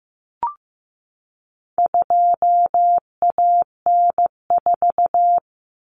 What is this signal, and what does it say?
Morse code, heard as a keyed tone, about 15 wpm, 700 Hz. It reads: 2AN4
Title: Common Cut Numbers - 4-Digit 15wpm